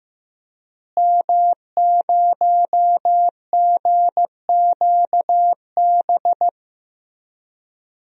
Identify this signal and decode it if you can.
Morse code, heard as a keyed tone, about 15 wpm, 700 Hz. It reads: M0GQB
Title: DXCC-100 Call Signs - 5 Characters - Repeated 3x - 15wpm